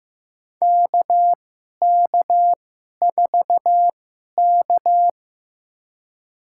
Morse code, heard as a keyed tone, about 15 wpm, spaced as for 11 wpm, 700 Hz. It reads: KK4K
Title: USA Call Signs - 4 Characters - 2x Character Spacing - 15wpm